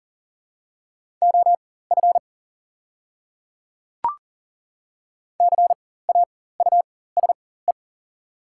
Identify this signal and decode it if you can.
Morse code, heard as a keyed tone, about 40 wpm, spaced as for 19 wpm, 700 Hz. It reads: OF CAUSE
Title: 4x Character Spacing - Top 300 Words - 40wpm